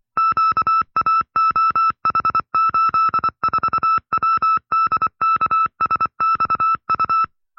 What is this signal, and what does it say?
Morse code, heard as a keyed tone, about 24 wpm, 1310 Hz. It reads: QAOH84WDKSXU
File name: FreeFiles / CW-practice / 20WPM-side2